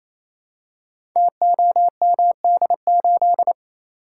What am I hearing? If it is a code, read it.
Morse code, heard as a keyed tone, about 27 wpm, 700 Hz. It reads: TOMD8
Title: Code Groups - 5 Letter:Numbers - 28wpm